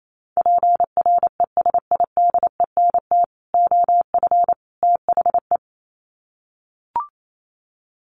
Morse code, heard as a keyed tone, about 28 wpm, 700 Hz. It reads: PRESIDENT OF THE
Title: Sets of 3 Words 28wpm